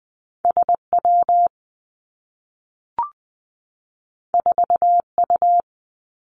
Morse code, heard as a keyed tone, about 20 wpm, 700 Hz. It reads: S W 4 U
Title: Two Challenging Letter-Number Combinations 20wpm